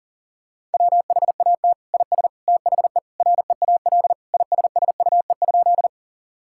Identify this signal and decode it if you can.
Morse code, heard as a keyed tone, about 40 wpm, 700 Hz. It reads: WHAT IS THE REAL ISSUE?
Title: Sentences from Top 1000 Words - Review Part 6 - 40wpm